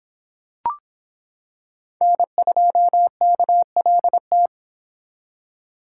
Morse code, heard as a keyed tone, about 25 wpm, 700 Hz. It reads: N2KLT